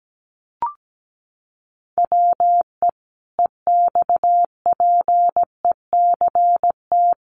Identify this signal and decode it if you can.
Morse code, heard as a keyed tone, about 17 wpm, 700 Hz. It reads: WE EXPECT
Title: Sentences from Top 1100 Words 17wpm